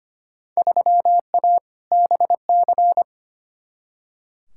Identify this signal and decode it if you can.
Morse code, heard as a keyed tone, about 25 wpm, 700 Hz. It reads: 3A BC
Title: Field Day Exchanges - 25wpm